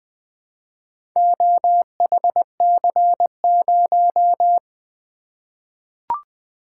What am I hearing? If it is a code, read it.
Morse code, heard as a keyed tone, about 20 wpm, 700 Hz. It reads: OHC0
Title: Four Letter-Number Combinations - 20wpm